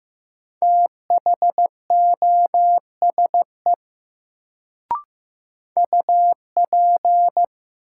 Morse code, heard as a keyed tone, about 15 wpm, 700 Hz. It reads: THOSE UP
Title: Sets of 2 Words - New Word Drill - 15wpm